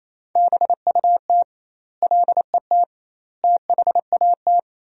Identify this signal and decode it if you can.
Morse code, heard as a keyed tone, about 28 wpm, 700 Hz. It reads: BUT LET THAT
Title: 2x Word-Spacing - Sentences from Top 400 Words - 28wpm